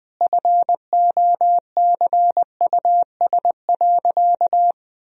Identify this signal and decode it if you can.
Morse code, heard as a keyed tone, about 20 wpm, 700 Hz. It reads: FOCUS.
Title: Sentences from Top 1000 Words - Review Part 4 - 20wpm